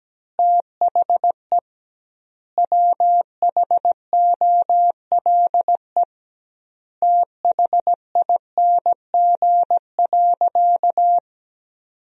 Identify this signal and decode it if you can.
Morse code, heard as a keyed tone, about 17 wpm, 700 Hz. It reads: THE WHOLE THING.
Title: 2x Word-Spacing - Sentences from Top 400 Words - 17wpm